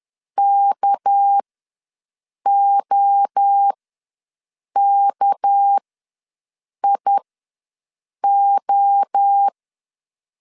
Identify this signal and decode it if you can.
Morse code, heard as a keyed tone, about 10 wpm, spaced as for 6 wpm, 800 Hz. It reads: KOKIO
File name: LEZIONI / 6 PKX0 / 6 PK ws 10wpm 3745